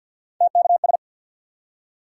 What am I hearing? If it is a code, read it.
Morse code, heard as a keyed tone, about 52 wpm, 700 Hz. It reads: TKS
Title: Top 90 QSO Elements 50wpm